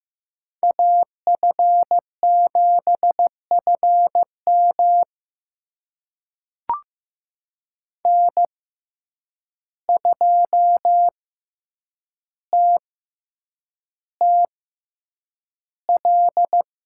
Morse code, heard as a keyed tone, about 15 wpm, spaced as for 8 wpm, 700 Hz. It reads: AF7FM N2TTL
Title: USA Call Signs - 5 Characters - 6x Character Spacing - 15wpm